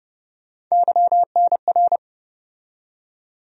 Morse code, heard as a keyed tone, about 31 wpm, 700 Hz. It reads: YNR
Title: USA Call Sign Postfixes - 3 Letters 30wpm